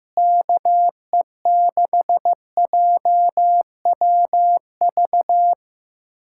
Morse code, heard as a keyed tone, about 15 wpm, 700 Hz. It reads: KE6JWV
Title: USA Callsign DE Callsign 15wpm